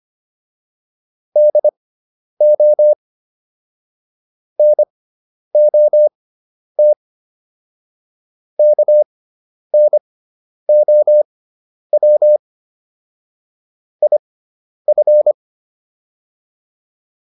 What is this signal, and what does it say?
Morse code, heard as a keyed tone, about 25 wpm, spaced as for 10 wpm, 600 Hz. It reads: DO NOT KNOW IF
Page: intermediate-practice-files